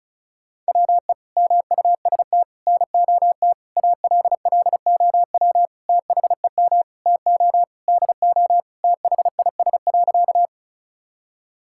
Morse code, heard as a keyed tone, about 36 wpm, 700 Hz. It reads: WE MUST NOT ALLOW THEM TO DO THIS.
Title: Sentences from Top 1000 Words - Review Part 5 - 35wpm